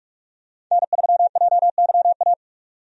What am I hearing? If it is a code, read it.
Morse code, heard as a keyed tone, about 46 wpm, 700 Hz. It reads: N3JYA
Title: USA Call Signs - 5 Characters - 45wpm